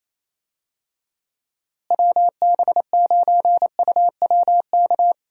Morse code, heard as a keyed tone, about 28 wpm, 700 Hz. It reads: WB9UWK
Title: USA Call Signs - 6 Characters - Repeated 2x - 28wpm